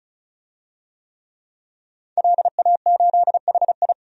Morse code, heard as a keyed tone, about 34 wpm, 700 Hz. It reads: LA8HI
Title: DXCC-100 Call Signs - 5 Characters - 35wpm